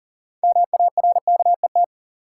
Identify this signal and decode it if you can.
Morse code, heard as a keyed tone, about 40 wpm, 700 Hz. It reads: MARKET
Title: Sentences from Top 600 Words - New Word Drill - 40wpm